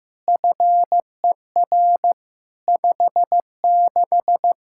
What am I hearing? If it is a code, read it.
Morse code, heard as a keyed tone, about 15 wpm, 700 Hz. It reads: FER 56
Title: Practice QSO - Exchange 3 - More Info - 15wpm